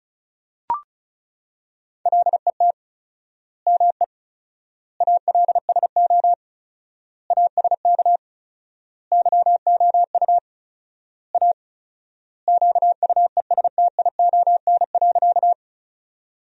Morse code, heard as a keyed tone, about 34 wpm, 700 Hz. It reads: LET ME ALSO ASK YOU A QUESTION.
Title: 4x Word-Spacing - Sentences from Top 400 Words - 35wpm